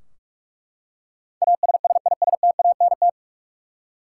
Morse code, heard as a keyed tone, about 41 wpm, 700 Hz. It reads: ASSISTANT
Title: Practice QSO - Exchange 3 - More Info - 45wpm